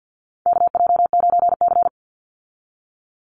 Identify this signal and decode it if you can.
Morse code, heard as a keyed tone, about 50 wpm, 700 Hz. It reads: KJ9C
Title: Call Signs - 50wpm